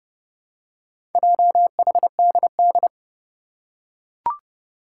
Morse code, heard as a keyed tone, about 30 wpm, 700 Hz. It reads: JHDD
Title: Four Challenging Letter-Number Combinations 30wpm